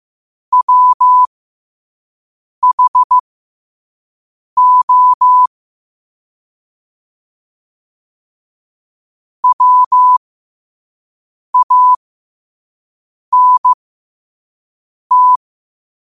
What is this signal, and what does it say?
Morse code, heard as a keyed tone, about 15 wpm, spaced as for 5 wpm, 1000 Hz. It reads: WHO WANT
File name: qotdfiles / 5wpm